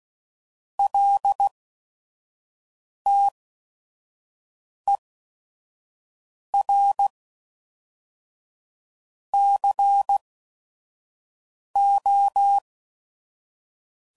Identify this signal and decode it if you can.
Morse code, heard as a keyed tone, about 16 wpm, spaced as for 5 wpm, 800 Hz. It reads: LTER CO
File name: CW Code Groups @5WPM